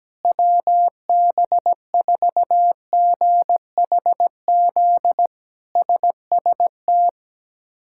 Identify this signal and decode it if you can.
Morse code, heard as a keyed tone, about 17 wpm, 700 Hz. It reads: WB4GHZ SST
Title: K1USN SST QSO Practice - 17wpm